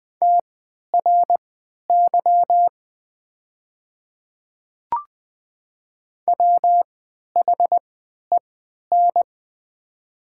Morse code, heard as a keyed tone, about 20 wpm, spaced as for 11 wpm, 700 Hz. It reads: TRY WHEN
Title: 3x Character Spacing - Top 200 Words - 20wpm